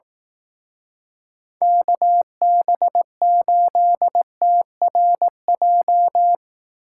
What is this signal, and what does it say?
Morse code, heard as a keyed tone, about 18 wpm, 700 Hz. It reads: KB8TRJ
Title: Speed Racing - USA 2x3 Call Signs 15wpm